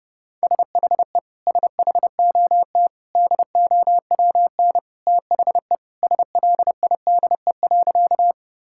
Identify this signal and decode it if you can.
Morse code, heard as a keyed tone, about 28 wpm, 700 Hz. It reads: SHE SHOT DOWN THE SLIDE.
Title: Sentences from Top 1600 Words 30wpm